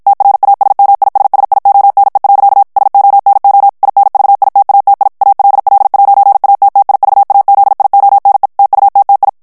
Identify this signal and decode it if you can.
Morse code, heard as a keyed tone, about 56 wpm, 800 Hz. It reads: TRASMISSIONE. SONO INVITATI ALL'ATTIVAZIONE TUTTI
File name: ITALIEN texte clair 1 vitesse 50 mots minute